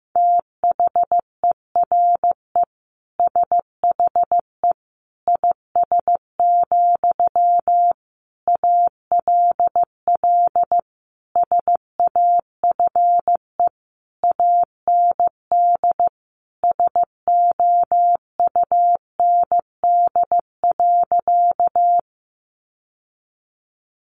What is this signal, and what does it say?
Morse code, heard as a keyed tone, about 15 wpm, 700 Hz. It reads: THERE SHE IS, ALL SAFE AND SOUND.
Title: Sentences from Top 900 Words 15wpm